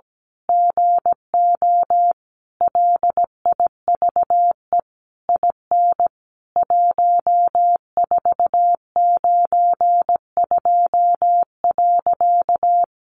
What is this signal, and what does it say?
Morse code, heard as a keyed tone, about 17 wpm, 700 Hz. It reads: GO LIVE IN 1492.
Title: A Sound Of Thunder 17wpm